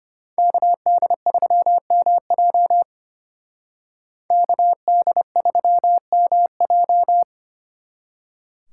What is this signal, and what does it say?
Morse code, heard as a keyed tone, about 32 wpm, 700 Hz. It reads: KD3MJ KD3MJ